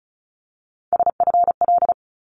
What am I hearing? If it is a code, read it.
Morse code, heard as a keyed tone, about 34 wpm, 700 Hz. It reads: SFL
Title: ARRL Sections 35wpm